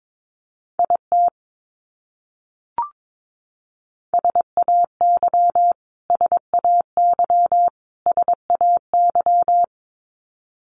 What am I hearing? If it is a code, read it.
Morse code, heard as a keyed tone, about 22 wpm, 700 Hz. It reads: IT SAY SAY SAY
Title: Top 300 Words Repeated 22wpm